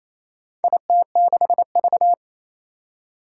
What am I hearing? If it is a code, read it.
Morse code, heard as a keyed tone, about 27 wpm, 700 Hz. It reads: IT6V
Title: Four Challenging Letter-Number Combinations 28wpm